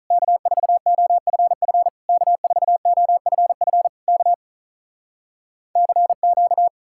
Morse code, heard as a keyed tone, about 46 wpm, 700 Hz. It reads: K4OFF K4OFF K CQ